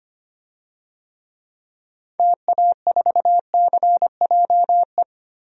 Morse code, heard as a keyed tone, about 25 wpm, 700 Hz. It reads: TA4CJE
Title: DXCC-100 Call Signs - 6 Characters - 25wpm